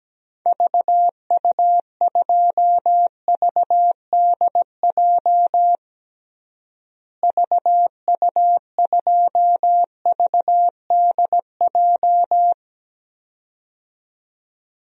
Morse code, heard as a keyed tone, about 17 wpm, 700 Hz. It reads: VU2VDJ VU2VDJ
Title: DXCC-100 Call Signs - 6 Characters - Repeated 2x - 17wpm